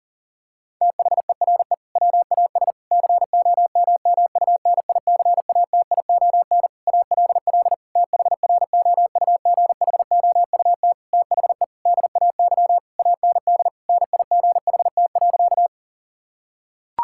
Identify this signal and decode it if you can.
Morse code, heard as a keyed tone, about 40 wpm, 690 Hz. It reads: THERE WAS COMMUNICATION ALL THROUGHOUT THE DAY AND NIGHT.
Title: Sentences from Top 1000 Words - Review Part 6 - 40wpm